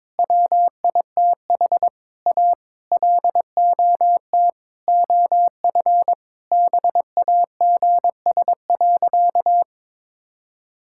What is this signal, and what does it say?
Morse code, heard as a keyed tone, about 22 wpm, 700 Hz. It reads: WITH A LOT OF BAGS.
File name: practice / 22wpm / Sentences with 4-Word Phrases - 22wpm